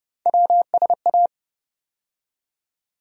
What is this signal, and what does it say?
Morse code, heard as a keyed tone, about 30 wpm, 700 Hz. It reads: WSA
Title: USA Call Sign Postfixes - 3 Letters 30wpm